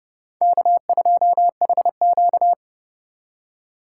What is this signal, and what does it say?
Morse code, heard as a keyed tone, about 28 wpm, 700 Hz. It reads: K2HQ